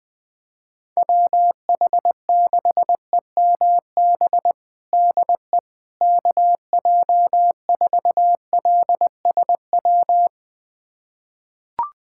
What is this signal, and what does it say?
Morse code, heard as a keyed tone, about 20 wpm, 700 Hz. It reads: WH6EMB DE KJ4LSW
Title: USA Callsign DE Callsign 20wpm